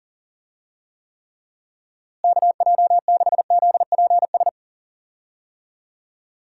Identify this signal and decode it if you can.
Morse code, heard as a keyed tone, about 40 wpm, 700 Hz. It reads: KJ6ZPS
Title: USA Call Signs - 6 Characters - Repeated 2x - 40wpm